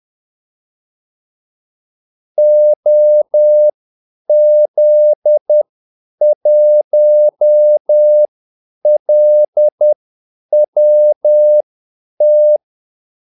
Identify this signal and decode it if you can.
Morse code, heard as a keyed tone, about 10 wpm, spaced as for 8 wpm, 600 Hz. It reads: OZ1LWT
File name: Callsigns-10-8wpm